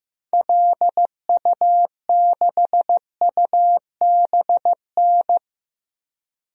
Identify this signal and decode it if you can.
Morse code, heard as a keyed tone, about 15 wpm, 700 Hz. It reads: LU6UBN
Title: DXCC-100 Call Signs - 6 Characters - 15wpm